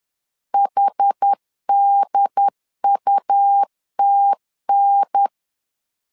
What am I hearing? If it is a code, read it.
Morse code, heard as a keyed tone, about 10 wpm, 800 Hz. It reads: HDUTN
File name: LEZIONI / 4 Q 10wpm 2210